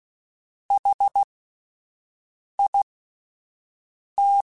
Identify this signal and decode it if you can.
Morse code, heard as a keyed tone, about 16 wpm, spaced as for 5 wpm, 800 Hz. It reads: HIT
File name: Alphabetical / Alphabetical mp3s / Hs @5